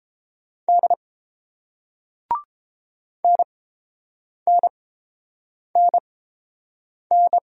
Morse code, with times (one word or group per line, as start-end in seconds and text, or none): 0.7-0.9 D
3.2-7.4 NNNN